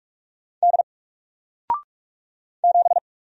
Morse code, heard as a keyed tone, about 44 wpm, 700 Hz. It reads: D 7
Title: Single Letter-Number Alt 45wpm